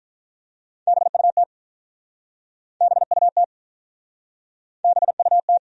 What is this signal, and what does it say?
Morse code, heard as a keyed tone, about 47 wpm, 700 Hz. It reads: BUT BUT BUT